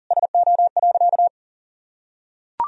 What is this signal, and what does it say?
Morse code, heard as a keyed tone, about 40 wpm, 700 Hz. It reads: SO.
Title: Sentences from Top 1000 Words - Review Part 2 - 40wpm